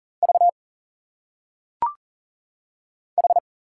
Morse code, heard as a keyed tone, about 40 wpm, 700 Hz. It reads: V H
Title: Single Letters - Phonetic Pronunciation 40wpm